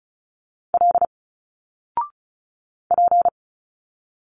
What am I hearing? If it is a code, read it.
Morse code, heard as a keyed tone, about 34 wpm, 700 Hz. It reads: L P
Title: Single Letters 35wpm